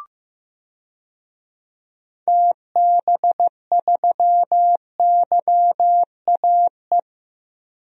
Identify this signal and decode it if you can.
Morse code, heard as a keyed tone, about 15 wpm, 700 Hz. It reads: TB3YAE
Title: DXCC-100 Call Signs - 6 Characters - 15wpm